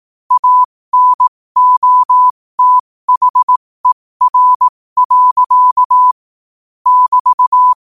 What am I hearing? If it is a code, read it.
Morse code, heard as a keyed tone, about 18 wpm, spaced as for 15 wpm, 1000 Hz. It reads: ANOTHER. <BT>
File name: qotdfiles / 15wpm